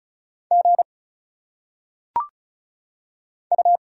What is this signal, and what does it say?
Morse code, heard as a keyed tone, about 34 wpm, 700 Hz. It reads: G U
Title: Single Letters - Phonetic Pronunciation 35wpm